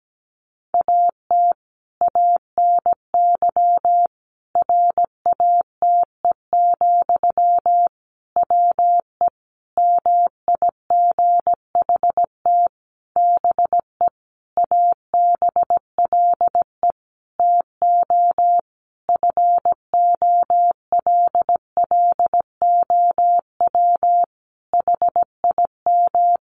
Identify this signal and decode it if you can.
Morse code, heard as a keyed tone, about 17 wpm, 700 Hz. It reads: AT ANY RATE, WE MIGHT BE ABLE TO FOLLOW HIM
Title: Sentences from Top 700 Words 17wpm